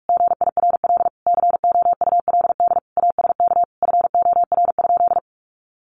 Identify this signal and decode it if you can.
Morse code, heard as a keyed tone, about 44 wpm, 700 Hz. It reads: GIRL COULD ASK FOR?
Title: Sentences from Top 1400 Words 45wpm